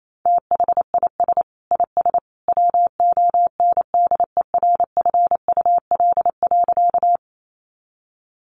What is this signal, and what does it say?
Morse code, heard as a keyed tone, about 28 wpm, 700 Hz. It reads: THIS IS WONDERFUL.
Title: Sentences from Top 1100 Words 28wpm